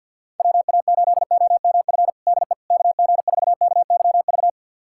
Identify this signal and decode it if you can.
Morse code, heard as a keyed tone, about 48 wpm, 700 Hz. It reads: WA8OMF DE KG4KYV
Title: USA Callsign DE Callsign - 6 Characters - 50wpm